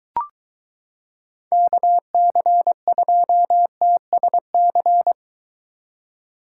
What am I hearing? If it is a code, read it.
Morse code, heard as a keyed tone, about 23 wpm, 700 Hz. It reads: KC2TSC